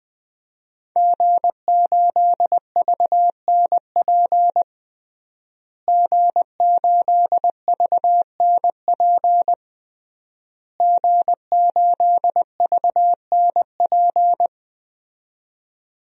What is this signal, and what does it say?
Morse code, heard as a keyed tone, about 20 wpm, 700 Hz. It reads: G8VNP G8VNP G8VNP
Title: DXCC-100 Call Signs - 5 Characters - Repeated 3x - 20wpm